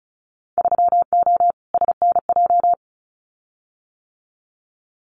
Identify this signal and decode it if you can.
Morse code, heard as a keyed tone, about 34 wpm, 700 Hz. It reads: 3O SNJ
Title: Winter Field Day Exchanges Encore 35wpm